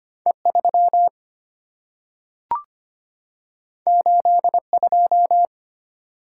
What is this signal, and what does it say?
Morse code, heard as a keyed tone, about 25 wpm, 700 Hz. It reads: E3 82
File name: Two Letter-Number Combinations - 25wpm